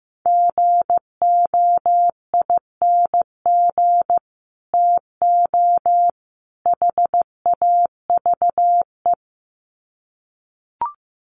Morse code, heard as a keyed tone, about 15 wpm, 700 Hz. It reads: GOING TO HAVE
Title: Sets of 3 Words 15wpm